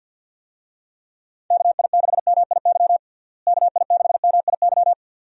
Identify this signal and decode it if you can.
Morse code, heard as a keyed tone, about 49 wpm, 700 Hz. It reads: KI6MIY KI6MIY
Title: USA Call Signs - 6 Characters - Repeated 2x - 50wpm